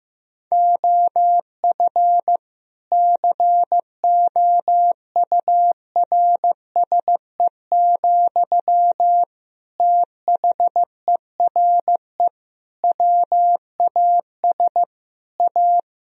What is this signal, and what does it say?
Morse code, heard as a keyed tone, about 15 wpm, 700 Hz. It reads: OF COURSE, THERE WAS A
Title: Sentences from Top 1000 Words - Review Part 4 - 15wpm